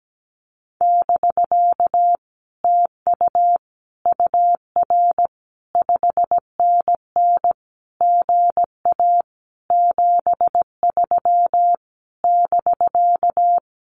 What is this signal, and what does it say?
Morse code, heard as a keyed tone, about 17 wpm, 700 Hz. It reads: <BK> TU UR 5NN GA 73 <BK>
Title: Common QSO Phrases Encore 17wpm